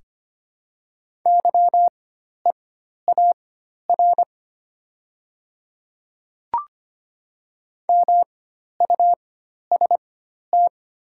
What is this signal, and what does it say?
Morse code, heard as a keyed tone, about 25 wpm, spaced as for 12 wpm, 700 Hz. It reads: YEAR MUST
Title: 4x Character Spacing - Top 200 Words - 25wpm